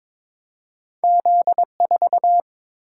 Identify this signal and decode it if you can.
Morse code, heard as a keyed tone, about 22 wpm, 690 Hz. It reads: Z4
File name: practice / 22wpm / Two Letter-Number Combinations - 22wpm